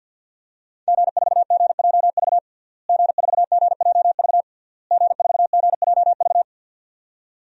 Morse code, heard as a keyed tone, about 51 wpm, 700 Hz. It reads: G4GJV G4GJV G4GJV
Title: DXCC-100 Call Signs - 5 Characters - Repeated 3x - 50wpm